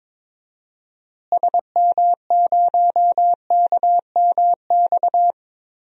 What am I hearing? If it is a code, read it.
Morse code, heard as a keyed tone, about 22 wpm, 700 Hz. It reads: SM0KMX
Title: DXCC-100 Call Signs - 6 Characters - Repeated 2x - 22wpm